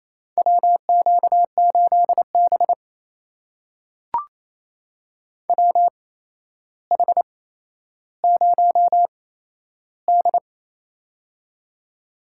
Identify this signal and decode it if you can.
Morse code, heard as a keyed tone, about 28 wpm, spaced as for 12 wpm, 700 Hz. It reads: WQ8B WH0D
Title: USA Call Signs - 4 Characters - 8x Character Spacing - 28wpm